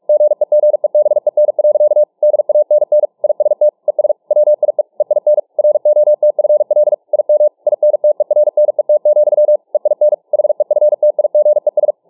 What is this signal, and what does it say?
Morse code, heard as a keyed tone, about 43 wpm, 600 Hz. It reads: GEGEBEN. DANN IST ES WIE EIN ROTFL IM INTERNET, EIN HEFTIGES